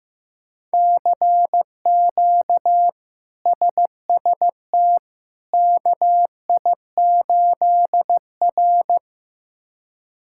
Morse code, heard as a keyed tone, about 15 wpm, 700 Hz. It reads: CQ SST KI8R
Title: K1USN SST QSO Practice - 15wpm